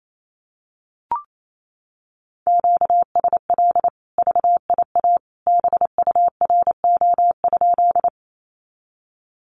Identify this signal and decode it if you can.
Morse code, heard as a keyed tone, about 28 wpm, 700 Hz. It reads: QSL VIA BURO?
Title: Common QSO Phrases 28wpm